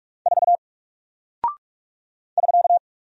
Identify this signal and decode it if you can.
Morse code, heard as a keyed tone, about 44 wpm, 700 Hz. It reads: 4 <SK>
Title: Single Character (Letter:Number:Punctuation:Prosign) Alt 45wpm